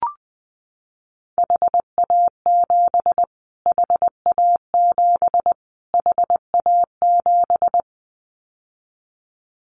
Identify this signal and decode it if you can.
Morse code, heard as a keyed tone, about 20 wpm, 700 Hz. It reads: HA7 HA7 HA7
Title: Callsign Prefixes for Top 25 DXCC - Repeated 3x - 20wpm